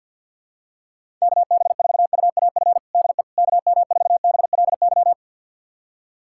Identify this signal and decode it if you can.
Morse code, heard as a keyed tone, about 50 wpm, 700 Hz. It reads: KB4UAF DE KM4BLY
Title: USA Callsign DE Callsign 50wpm